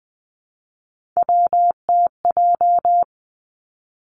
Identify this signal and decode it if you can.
Morse code, heard as a keyed tone, about 20 wpm, 700 Hz. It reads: WTJ
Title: Three Challenging Letter-Number Combinations 20wpm